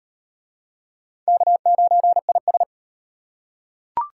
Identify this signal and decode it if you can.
Morse code, heard as a keyed tone, about 38 wpm, 700 Hz. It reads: K9IS